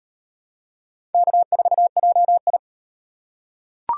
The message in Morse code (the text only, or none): K4JI